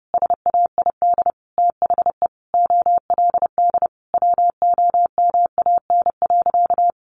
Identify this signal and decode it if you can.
Morse code, heard as a keyed tone, about 30 wpm, 700 Hz. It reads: SAID THE OLD WOMAN.
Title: Sentences from Top 1100 Words 30wpm